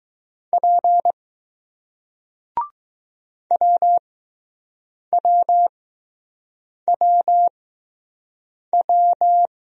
Morse code, heard as a keyed tone, about 20 wpm, spaced as for 6 wpm, 700 Hz. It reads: P WWWW